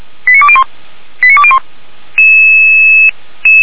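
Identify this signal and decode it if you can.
Five-tone selective call (ZVEI), heard as quick runs of five tones, tones of about 70 ms. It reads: 89201 89201
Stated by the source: Five-tone ZVEI selective call received by SDR.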